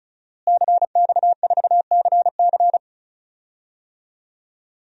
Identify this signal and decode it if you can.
Morse code, heard as a keyed tone, about 34 wpm, 700 Hz. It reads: CX4CC
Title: DXCC-100 Call Signs - 5 Characters - Repeated 2x - 35wpm